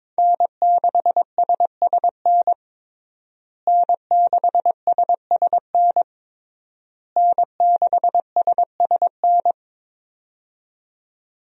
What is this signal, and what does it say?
Morse code, heard as a keyed tone, about 22 wpm, 700 Hz. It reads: N6SSN N6SSN N6SSN
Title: USA Call Signs - 5 Characters - Repeated 3x - 22wpm